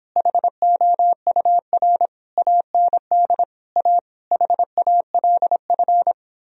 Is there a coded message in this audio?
Yes: Morse code from 0.2 to 6.1 s, heard as a keyed tone, about 29 wpm, 700 Hz. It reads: HOUR AND A HALF